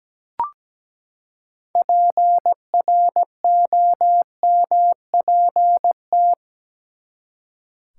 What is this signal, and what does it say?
Morse code, heard as a keyed tone, about 17 wpm, 700 Hz. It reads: PROMPT
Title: Six Letter Words - 17wpm